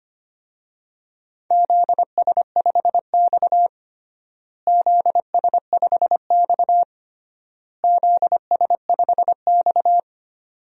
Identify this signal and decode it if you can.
Morse code, heard as a keyed tone, about 25 wpm, 700 Hz. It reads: ZS5X ZS5X ZS5X
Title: DXCC-100 Call Signs - 4 Characters - Repeated 3x - 25wpm